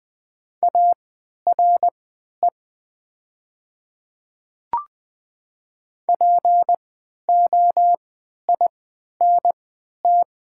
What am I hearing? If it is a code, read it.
Morse code, heard as a keyed tone, about 20 wpm, spaced as for 11 wpm, 700 Hz. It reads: ARE POINT
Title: 3x Character Spacing - Top 200 Words - 20wpm